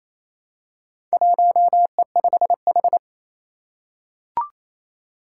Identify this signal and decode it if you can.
Morse code, heard as a keyed tone, about 27 wpm, 700 Hz. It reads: 1E5H
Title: Four Challenging Letter-Number Combinations 28wpm